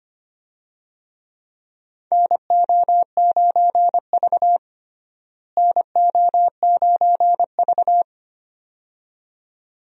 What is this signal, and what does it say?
Morse code, heard as a keyed tone, about 25 wpm, 700 Hz. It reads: NO9V NO9V
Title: USA Call Signs - 4 Characters - Repeated 2x - 25wpm